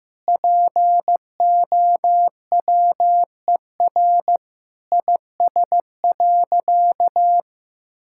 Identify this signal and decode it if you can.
Morse code, heard as a keyed tone, about 15 wpm, 700 Hz. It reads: POWER IS.
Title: Sentences from Top 1000 Words - Review Part 6 - 15wpm